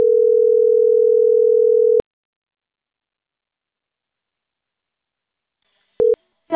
Ringback tone (North America), 440 + 480 Hz.